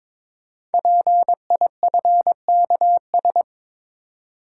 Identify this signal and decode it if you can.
Morse code, heard as a keyed tone, about 22 wpm, 700 Hz. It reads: PIFKS
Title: Code Groups - 5 Letters - 22wpm